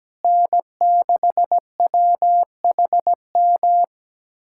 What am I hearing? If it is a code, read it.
Morse code, heard as a keyed tone, about 17 wpm, 700 Hz. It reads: N6WHM
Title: USA Callsign DE Callsign - 5 Characters - 17wpm